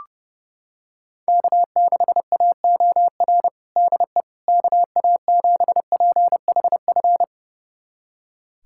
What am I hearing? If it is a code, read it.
Morse code, heard as a keyed tone, about 32 wpm, 700 Hz. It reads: K6AOR DE KA7PHF
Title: USA Callsign DE Callsign 30wpm